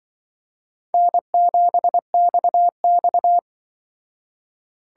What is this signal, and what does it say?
Morse code, heard as a keyed tone, about 24 wpm, 700 Hz. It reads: N7XX
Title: Speed Racing - USA 1x2 Call Signs 20wpm